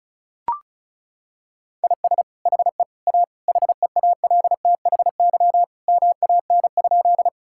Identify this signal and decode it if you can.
Morse code, heard as a keyed tone, about 34 wpm, 700 Hz. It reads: IS HE A HEALTHY MAN?
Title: Sentences from Top 1000 Words - Review Part 5 - 35wpm